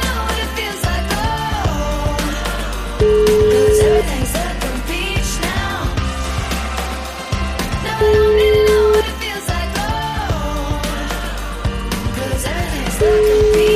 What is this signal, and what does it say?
Ringback tone (Europe, 425 Hz), with noise.